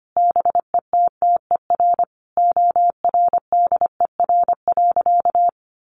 Morse code, heard as a keyed tone, about 25 wpm, 700 Hz. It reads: BETTER ORDER.
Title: Sentences from Top 1500 Words 25wpm